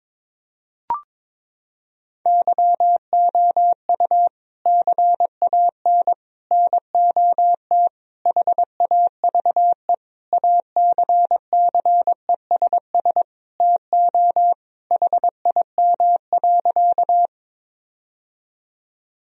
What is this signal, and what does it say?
Morse code, heard as a keyed tone, about 22 wpm, 700 Hz. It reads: YOU CAN NOT HAVE ACCESS TO HIM.
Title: Sentences from Top 1000 Words - Review Part 4 - 22wpm